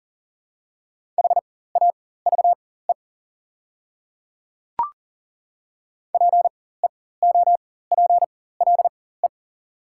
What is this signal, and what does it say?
Morse code, heard as a keyed tone, about 40 wpm, spaced as for 19 wpm, 690 Hz. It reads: HAVE PEOPLE
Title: 4x Character Spacing - Top 100 Words - 40wpm